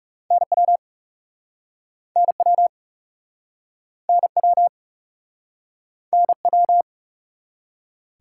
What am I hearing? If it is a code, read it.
Morse code, heard as a keyed tone, about 38 wpm, 700 Hz. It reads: NW NW NW NW